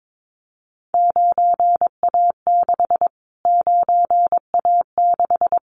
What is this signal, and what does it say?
Morse code, heard as a keyed tone, about 22 wpm, 700 Hz. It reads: 9A6 9A6
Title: Callsign Prefixes for Top 25 DXCC - Repeated 3x - 22wpm